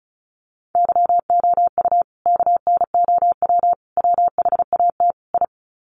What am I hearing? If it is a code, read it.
Morse code, heard as a keyed tone, about 34 wpm, 700 Hz. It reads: YOU KNOW WHAT I
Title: Sets of 4 Words 35wpm